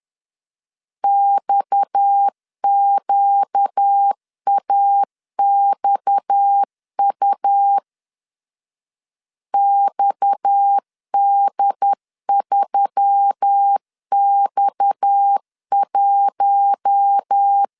Morse code, heard as a keyed tone, about 10 wpm, 800 Hz. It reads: XQAXU XD3X1
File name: LEZIONI / 6 PKX0 / 6 PKX 10wpm 2426